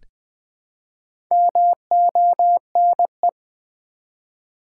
Morse code, heard as a keyed tone, about 20 wpm, 700 Hz. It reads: MONE
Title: Build A Word - 1 Letter At A Time - 3 to 5-Letter Words - 20wpm